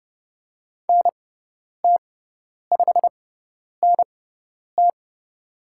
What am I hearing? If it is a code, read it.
Morse code, heard as a keyed tone, about 30 wpm, spaced as for 10 wpm, 700 Hz. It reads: NT5NT
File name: Lesson 18g - Letter m - Call Signs 30wpm